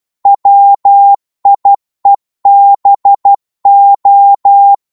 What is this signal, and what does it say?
Morse code, heard as a keyed tone, about 12 wpm, 800 Hz. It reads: WIEBO
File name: Morse / ltr-12wpm